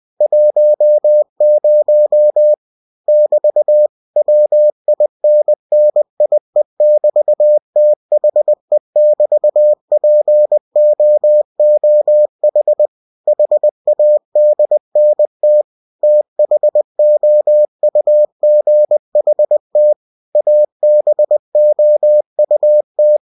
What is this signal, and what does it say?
Morse code, heard as a keyed tone, about 20 wpm, 600 Hz. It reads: 10 <BT> WINNIE<BT>THE<BT>POOH HADNT THOUGHT ABOUT